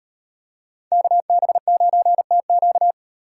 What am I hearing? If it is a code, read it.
Morse code, heard as a keyed tone, about 38 wpm, 700 Hz. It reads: KB9TQ